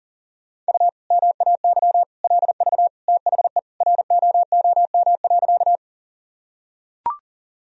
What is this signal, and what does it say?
Morse code, heard as a keyed tone, about 40 wpm, 700 Hz. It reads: U MAY LV THE ROOM.
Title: Sentences from Top 400 Words and Abbreviations 40wpm